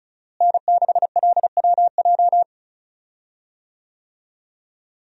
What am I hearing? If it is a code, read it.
Morse code, heard as a keyed tone, about 36 wpm, 700 Hz. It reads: N6LWJ
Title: USA Call Signs - 5 Characters - Repeated 3x - 35wpm